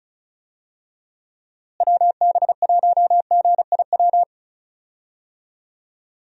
Morse code, heard as a keyed tone, about 36 wpm, 700 Hz. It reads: WB1GIW
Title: USA Call Signs - 6 Characters - Repeated 3x - 35wpm